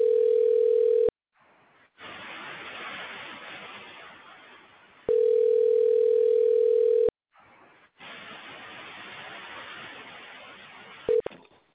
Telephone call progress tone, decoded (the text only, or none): ringback tone